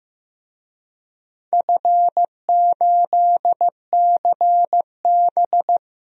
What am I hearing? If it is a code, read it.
Morse code, heard as a keyed tone, about 15 wpm, 700 Hz. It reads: F8CB
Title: DXCC-100 Call Signs - 4 Characters - 15wpm